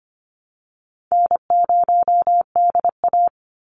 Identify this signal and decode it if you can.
Morse code, heard as a keyed tone, about 25 wpm, 700 Hz. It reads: N0DA
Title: NA SOTA Chaser Callsigns 25wpm